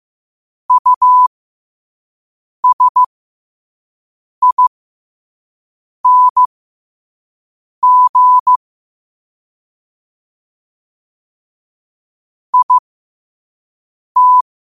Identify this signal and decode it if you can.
Morse code, heard as a keyed tone, about 15 wpm, spaced as for 5 wpm, 1000 Hz. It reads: USING IT
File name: qotdfiles / 5wpm